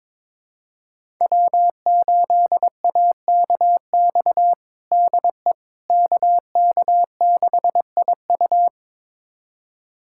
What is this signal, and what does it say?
Morse code, heard as a keyed tone, about 21 wpm, 700 Hz. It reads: W8AKX DE KK6IU
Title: USA Callsign DE Callsign - 5 Characters - 22wpm